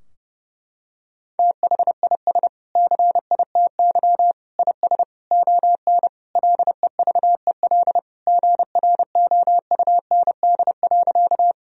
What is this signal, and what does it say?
Morse code, heard as a keyed tone, about 30 wpm, 700 Hz. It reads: THIS CITY IS ON LEVEL GROUND.
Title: Sentences from Top 1000 Words - Review Part 4 - 30wpm